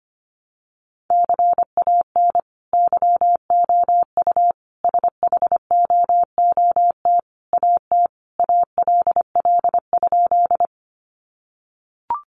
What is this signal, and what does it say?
Morse code, heard as a keyed tone, about 25 wpm, 700 Hz. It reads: CAN YOU SHOOT AT ALL?
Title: Sentences from Top 1200 Words 25wpm